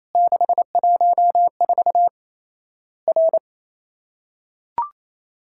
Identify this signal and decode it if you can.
Morse code, heard as a keyed tone, about 28 wpm, 700 Hz. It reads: 614 R
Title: Worked All Europe DX Contest - QTC Exchange - 28wpm